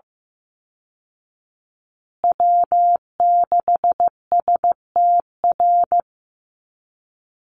Morse code, heard as a keyed tone, about 15 wpm, 700 Hz. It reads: W6STR
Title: Call Signs Repeated 3x - 15wpm